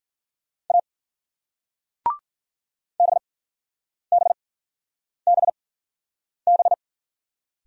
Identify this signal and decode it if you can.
Morse code, heard as a keyed tone, about 48 wpm, spaced as for 9 wpm, 700 Hz. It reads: A BBBB
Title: Speed-Racing Single Letters 40wpm